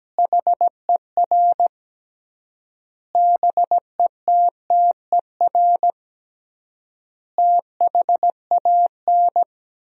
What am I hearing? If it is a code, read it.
Morse code, heard as a keyed tone, about 17 wpm, 700 Hz. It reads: HER BETTER THAN
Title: 3x Word-Spacing - Sentences from Top 400 Words - 17wpm